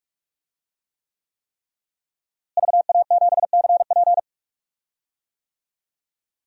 Morse code, heard as a keyed tone, about 44 wpm, 700 Hz. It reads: VA7CP
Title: DXCC-100 Call Signs - 5 Characters - Repeated 3x - 45wpm